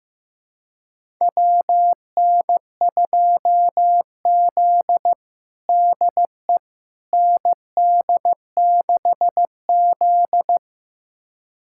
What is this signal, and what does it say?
Morse code, heard as a keyed tone, about 15 wpm, 700 Hz. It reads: WN2Z DE ND6Z
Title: USA Callsign DE Callsign - 4 Characters - 15wpm